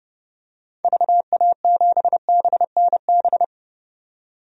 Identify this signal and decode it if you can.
Morse code, heard as a keyed tone, about 30 wpm, 700 Hz. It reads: VA7BNB
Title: DXCC-100 Call Signs - 6 Characters - 30wpm